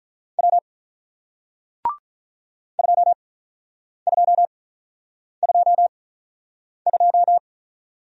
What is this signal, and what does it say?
Morse code, heard as a keyed tone, about 48 wpm, spaced as for 9 wpm, 700 Hz. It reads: W 2222